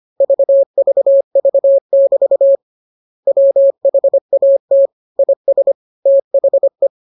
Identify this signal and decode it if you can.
Morse code, heard as a keyed tone, about 25 wpm, 550 Hz. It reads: VVV<BT> WHAT IS THE